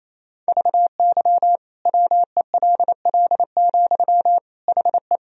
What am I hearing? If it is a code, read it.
Morse code, heard as a keyed tone, about 28 wpm, 700 Hz. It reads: VY WELL, HE
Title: Sentences from Top 200 Words and Abbreviations 28wpm